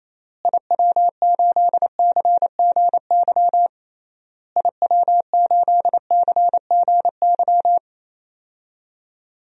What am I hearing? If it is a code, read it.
Morse code, heard as a keyed tone, about 27 wpm, 700 Hz. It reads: IW8CGY IW8CGY
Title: DXCC-100 Call Signs - 6 Characters - Repeated 3x - 28wpm